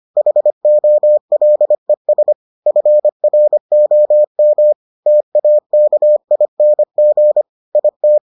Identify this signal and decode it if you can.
Morse code, heard as a keyed tone, about 25 wpm, 600 Hz. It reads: HOLES FROM TAKING IT